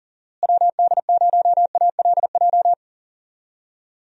Morse code, heard as a keyed tone, about 40 wpm, 700 Hz. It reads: WD0ALJ
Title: USA Call Signs - 6 Characters - Repeated 3x - 40wpm